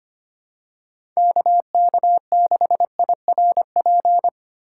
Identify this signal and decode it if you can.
Morse code, heard as a keyed tone, about 25 wpm, 700 Hz. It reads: KK6IRP
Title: USA Call Signs - 6 Characters - 3x Character Spacing - 25wpm